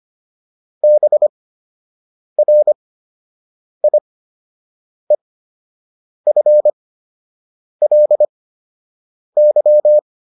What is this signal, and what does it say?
Morse code, heard as a keyed tone, about 25 wpm, spaced as for 7 wpm, 600 Hz. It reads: BRIEFLY